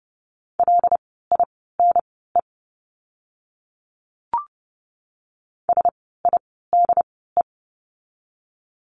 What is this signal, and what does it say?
Morse code, heard as a keyed tone, about 30 wpm, spaced as for 17 wpm, 700 Hz. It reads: LINE SIDE
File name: course / 30:10wpm / 115 Lesson 19f - Letter w - Words 30wpm